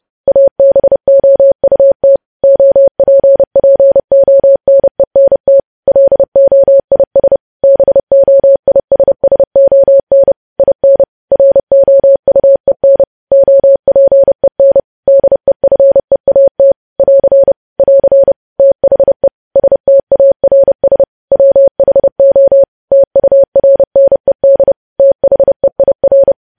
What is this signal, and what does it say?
Morse code, heard as a keyed tone, about 30 wpm, 550 Hz. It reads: ABOUT OPPONENT LOIS BOISSON IN ROUEN OPEN DEFEAT <AR> <AR> THE STARS WHO TURNED THEIR
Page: news headlines in morse code at 30 wpm